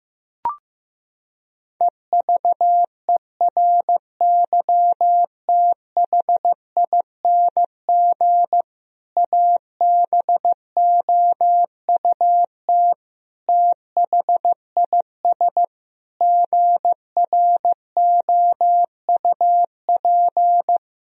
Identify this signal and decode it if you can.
Morse code, heard as a keyed tone, about 15 wpm, 700 Hz. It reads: EVERYTHING ABOUT THIS GROUP
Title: Sentences from Top 1000 Words - Review Part 1 - 15wpm